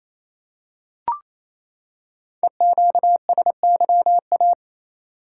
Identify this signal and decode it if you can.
Morse code, heard as a keyed tone, about 28 wpm, 700 Hz. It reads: EQSYA
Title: Code Groups - 5 Letters - 28wpm